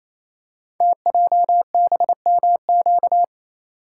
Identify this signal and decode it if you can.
Morse code, heard as a keyed tone, about 27 wpm, 700 Hz. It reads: TJBMQ
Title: Code Groups - 5 Letters - 28wpm